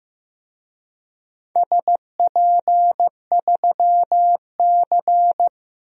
Morse code, heard as a keyed tone, about 15 wpm, 700 Hz. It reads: SP3C
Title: DXCC-100 Call Signs - 4 Characters - 15wpm